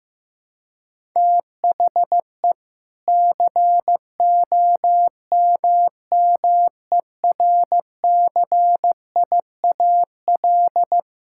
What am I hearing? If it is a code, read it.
Morse code, heard as a keyed tone, about 15 wpm, 700 Hz. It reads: THE COMMERCIAL
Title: Sentences from Top 1000 Words - Review Part 5 - 15wpm